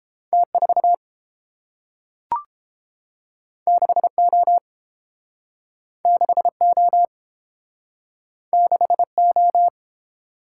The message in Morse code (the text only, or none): T4 6O 6O 6O